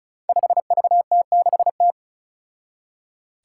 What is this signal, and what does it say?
Morse code, heard as a keyed tone, about 36 wpm, 700 Hz. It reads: 5VT6T
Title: Five Challenging Letter-Number Combinations 35wpm